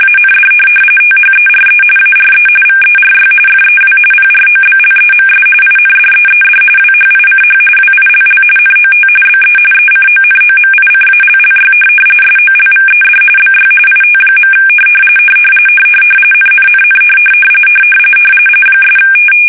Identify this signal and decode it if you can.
RTTY, about 45.45 baud. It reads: AND THERE ARE VERY FEW PROBLEMS INDEED WHICH WILL NOT BE ECONOMICALLY SOLVED BY IT. NNNN '
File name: fsk850